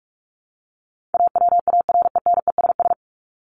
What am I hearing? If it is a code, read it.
Morse code, heard as a keyed tone, about 44 wpm, 700 Hz. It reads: AWARENESS
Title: Top 1501-1750 Words 45wpm